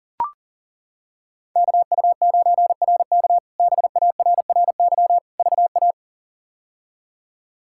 Morse code, heard as a keyed tone, about 40 wpm, 700 Hz. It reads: KU9RK BARRY VA
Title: K1USN SST QSO Practice - 40wpm